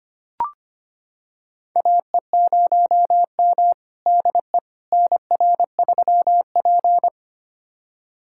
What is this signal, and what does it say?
Morse code, heard as a keyed tone, about 25 wpm, 700 Hz. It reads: AE0M DE NR3P
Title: USA Callsign DE Callsign - 4 Characters - 25wpm